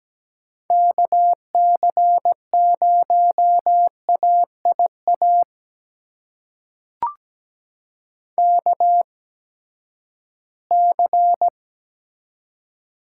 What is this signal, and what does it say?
Morse code, heard as a keyed tone, about 17 wpm, spaced as for 12 wpm, 700 Hz. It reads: KC0AIA KC
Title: USA Call Signs - 6 Characters - 8x Character Spacing - 17wpm